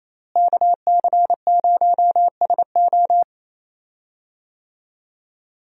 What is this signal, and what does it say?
Morse code, heard as a keyed tone, about 27 wpm, 700 Hz. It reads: KC0SO
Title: USA Call Signs - 5 Characters - Repeated 2x - 28wpm